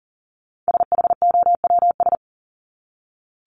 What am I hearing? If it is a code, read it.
Morse code, heard as a keyed tone, about 40 wpm, 700 Hz. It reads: SHOWS
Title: ES:S Suffix - Top 100 Words with Base Words 40wpm